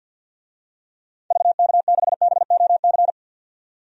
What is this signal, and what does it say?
Morse code, heard as a keyed tone, about 52 wpm, 700 Hz. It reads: VK6BOC
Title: DXCC-100 Call Signs - 6 Characters - 50wpm